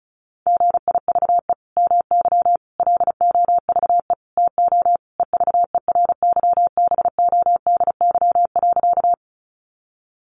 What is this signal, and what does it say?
Morse code, heard as a keyed tone, about 34 wpm, 700 Hz. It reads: GIVE MY LOVE TO EVERYBODY.
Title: Sentences from Top 1400 Words 35wpm